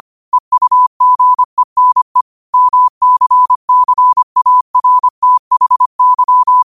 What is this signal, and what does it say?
Morse code, heard as a keyed tone, about 25 wpm, 1000 Hz. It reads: EUGENE MCCARTHY
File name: qotdfiles / 25wpm